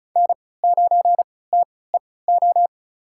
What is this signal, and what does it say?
Morse code, heard as a keyed tone, about 36 wpm, spaced as for 20 wpm, 700 Hz. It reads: N9TEO
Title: USA Call Signs - 5 Characters - 3x Character Spacing - 35wpm